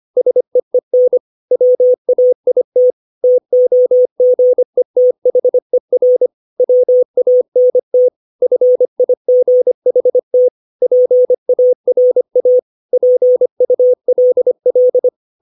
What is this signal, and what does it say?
Morse code, heard as a keyed tone, about 25 wpm, 500 Hz. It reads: SEEN WAIT TOGETHER WANT FIGHT PARA PULL
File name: 632 words / 632 Words 25WPM